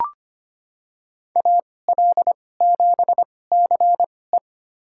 Morse code, heard as a keyed tone, about 25 wpm, spaced as for 18 wpm, 700 Hz. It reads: AL7CE
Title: USA Call Signs - 5 Characters - 2x Character Spacing - 25wpm